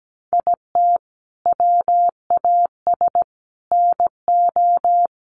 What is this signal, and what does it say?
Morse code, heard as a keyed tone, about 17 wpm, 700 Hz. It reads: IT WAS NO
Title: Sentences from Top 1500 Words 17wpm